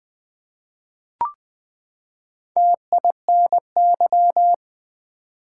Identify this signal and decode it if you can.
Morse code, heard as a keyed tone, about 20 wpm, 700 Hz. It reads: TINY
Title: Four Letter Words - 20wpm